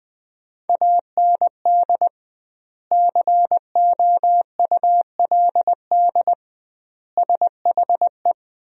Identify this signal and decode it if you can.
Morse code, heard as a keyed tone, about 21 wpm, 700 Hz. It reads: AND COULD SHE
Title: 2x Word-Spacing - Sentences from Top 100 Words - 20wpm